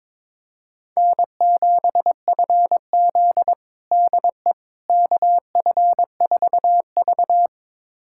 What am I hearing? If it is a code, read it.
Morse code, heard as a keyed tone, about 21 wpm, 700 Hz. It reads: N7FZ DE KF4V
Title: USA Callsign DE Callsign - 4 Characters - 22wpm